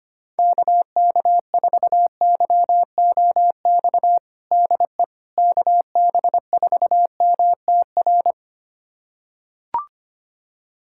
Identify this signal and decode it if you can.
Morse code, heard as a keyed tone, about 25 wpm, 700 Hz. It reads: KK4YOX DE KB4MTR
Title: USA Callsign DE Callsign 25wpm